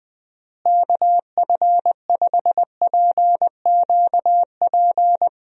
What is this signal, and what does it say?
Morse code, heard as a keyed tone, about 20 wpm, 700 Hz. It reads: KF5PQP
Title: USA Call Signs - 6 Characters - Repeated 2x - 20wpm